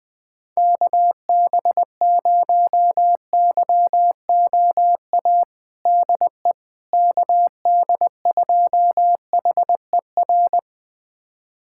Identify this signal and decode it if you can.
Morse code, heard as a keyed tone, about 20 wpm, 700 Hz. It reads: KB0YOA DE KD2HER
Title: USA Callsign DE Callsign - 6 Characters - 20wpm